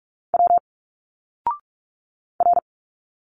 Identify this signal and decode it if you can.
Morse code, heard as a keyed tone, about 44 wpm, 700 Hz. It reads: W R
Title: Single Letters 45wpm